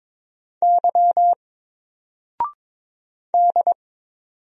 Morse code, heard as a keyed tone, about 22 wpm, 700 Hz. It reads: Y D
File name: Single-Letter-Number-22wpm